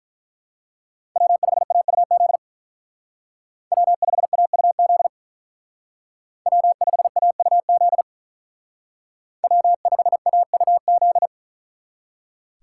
Morse code, heard as a keyed tone, about 43 wpm, 700 Hz. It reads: W5AUZ W5AUZ W5AUZ W5AUZ
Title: Speed Racing - USA Call Signs 35wpm